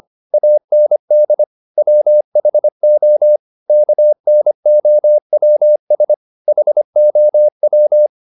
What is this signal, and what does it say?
Morse code, heard as a keyed tone, about 25 wpm, 600 Hz. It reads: AND WHO KNOWS HOW
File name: the old man and the sea part 1 25wpm